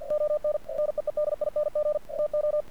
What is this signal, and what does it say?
Morse code, heard as a keyed tone, about 48 wpm, 600 Hz, with radio noise. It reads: ON NEEDING TO
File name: archived / CRSnet